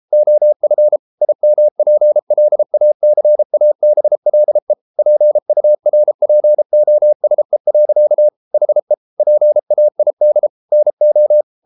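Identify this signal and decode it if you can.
Morse code, heard as a keyed tone, about 33 wpm, 600 Hz. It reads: OF IMPLACABLE PURPOSE. HE PAID NO
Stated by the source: Morse code practice at 32 wpm from CW med Gnister, reading Ken Follett's novel Never in English.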